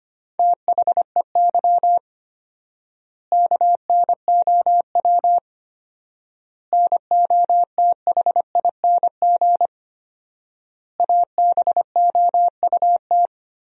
Morse code, heard as a keyed tone, about 25 wpm, 700 Hz. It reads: THEY KNOW NOTHING ABOUT
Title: 4x Word-Spacing - Sentences from Top 500 Words - 25wpm